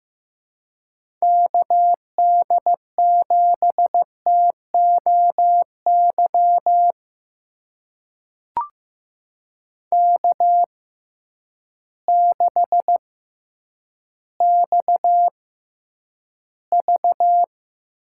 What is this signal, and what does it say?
Morse code, heard as a keyed tone, about 15 wpm, spaced as for 9 wpm, 700 Hz. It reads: KD7TOY K6XV
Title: USA Call Signs - 6x Character Spacing - 15wpm